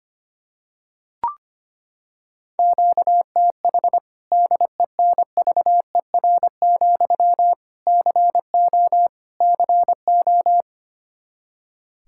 Morse code, heard as a keyed tone, about 25 wpm, 700 Hz. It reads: QTH DENVER, CO CO
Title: K1USN SST QSO Practice 25wpm